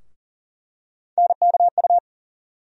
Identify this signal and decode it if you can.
Morse code, heard as a keyed tone, about 40 wpm, 700 Hz. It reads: NKU
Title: USA Call Sign Postfixes - 3 Letters 40wpm